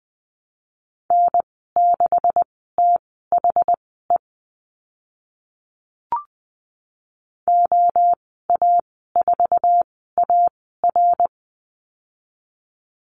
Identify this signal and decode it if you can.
Morse code, heard as a keyed tone, about 20 wpm, spaced as for 14 wpm, 700 Hz. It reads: N6THE OA4AR
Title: 140 Lesson 23g - Letter f - Call Signs 20wpm